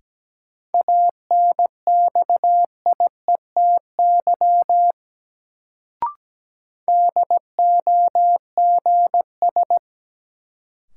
Morse code, heard as a keyed tone, about 17 wpm, 700 Hz. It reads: ANXIETY DOGS
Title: Sentences from Top 1600 Words - New Word Drill Alt - 17wpm